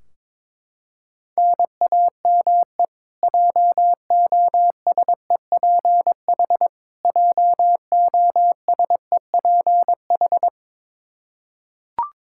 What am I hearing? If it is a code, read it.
Morse code, heard as a keyed tone, about 22 wpm, 700 Hz. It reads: NAME JOSEPH JOSEPH
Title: Practice QSO - Exchange 1 - Beginning - 22wpm